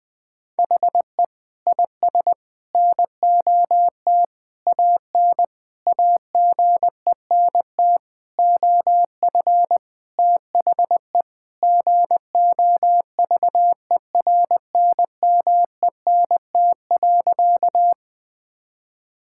Morse code, heard as a keyed tone, about 20 wpm, 700 Hz. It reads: HE IS NOT AN AGENT OF THE GOVERNMENT.
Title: Sentences from Top 1600 Words 20wpm